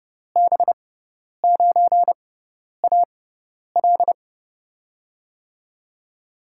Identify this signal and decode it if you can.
Morse code, heard as a keyed tone, about 30 wpm, spaced as for 10 wpm, 700 Hz. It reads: B9AL
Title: USA Call Signs - 5 Characters - 6x Character Spacing - 30wpm